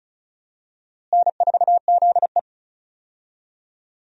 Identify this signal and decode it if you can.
Morse code, heard as a keyed tone, about 36 wpm, 700 Hz. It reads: N4ZE
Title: USA Call Signs - 4 Characters - Repeated 3x - 35wpm